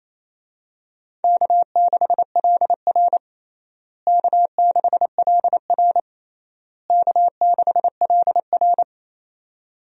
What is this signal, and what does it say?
Morse code, heard as a keyed tone, about 28 wpm, 700 Hz. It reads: K6LR K6LR K6LR
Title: USA Call Signs - 4 Characters - Repeated 3x - 28wpm